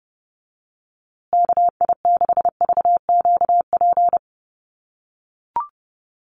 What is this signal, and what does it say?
Morse code, heard as a keyed tone, about 30 wpm, 700 Hz. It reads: KI6VQP
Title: USA Call Signs Repeated 3x 30wpm